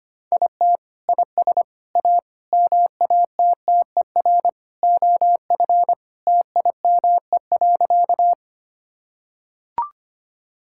Morse code, heard as a keyed tone, about 25 wpm, 700 Hz. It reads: IT IS A MATTER OF TIME.
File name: practice / 25wpm / Sentences with 3-Word Phrases - 25wpm